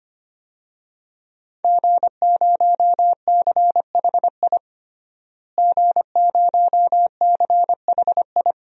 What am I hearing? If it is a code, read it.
Morse code, heard as a keyed tone, about 25 wpm, 700 Hz. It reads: G0CHI G0CHI
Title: DXCC-100 Call Signs - 5 Characters - Repeated 2x - 25wpm